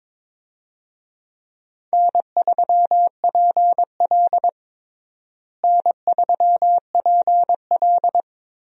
Morse code, heard as a keyed tone, about 22 wpm, 700 Hz. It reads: N3PL N3PL
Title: USA Call Signs - 4 Characters - Repeated 3x - 22wpm